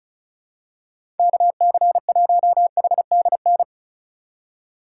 Morse code, heard as a keyed tone, about 36 wpm, 700 Hz. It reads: KC1HDN
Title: USA Call Signs - 6 Characters - 8x Character Spacing - 35wpm